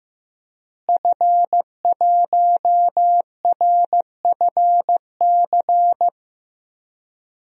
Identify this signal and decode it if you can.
Morse code, heard as a keyed tone, about 15 wpm, 700 Hz. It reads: F1RFC
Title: DXCC-100 Call Signs - 5 Characters - Repeated 3x - 15wpm